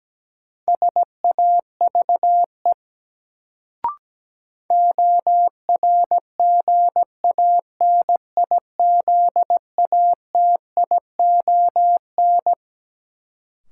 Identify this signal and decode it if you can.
Morse code, heard as a keyed tone, about 17 wpm, 700 Hz. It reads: SAVE ORGANIZATION
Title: Sentences from Top 1100 Words - New Word Drill Alt - 17wpm